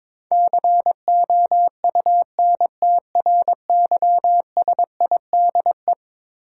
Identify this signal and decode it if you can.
Morse code, heard as a keyed tone, about 22 wpm, 700 Hz. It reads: COUNTRYSIDE
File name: practice / 22wpm / Eleven Letter Words - 22wpm